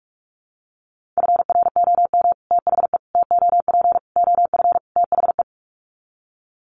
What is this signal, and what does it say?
Morse code, heard as a keyed tone, about 44 wpm, 700 Hz. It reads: FROM THE TOP OF THE
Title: Sets of 5 Words Encore 45wpm